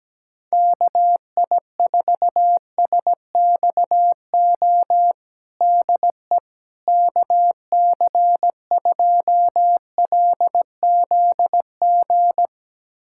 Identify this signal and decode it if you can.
Morse code, heard as a keyed tone, about 17 wpm, 700 Hz. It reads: KI4SXO DE KC2LZG
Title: USA Callsign DE Callsign 17wpm